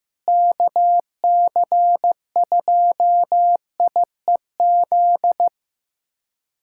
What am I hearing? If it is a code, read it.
Morse code, heard as a keyed tone, about 15 wpm, 700 Hz. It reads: KC2IEZ
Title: USA Callsign DE Callsign 15wpm